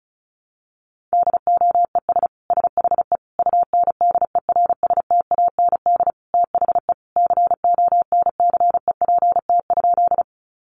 Morse code, heard as a keyed tone, about 34 wpm, 700 Hz. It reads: DOES SHE UNDERSTAND THE CONCEPT?
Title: Sentences from Top 1100 Words 35wpm